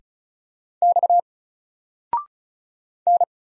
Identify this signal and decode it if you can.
Morse code, heard as a keyed tone, about 36 wpm, 700 Hz. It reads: X N
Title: Single Letter-Number 35wpm